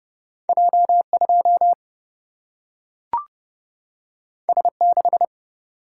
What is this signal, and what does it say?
Morse code, heard as a keyed tone, about 30 wpm, 700 Hz. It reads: J 2 S 6
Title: Single-Letter Single-Number 30wpm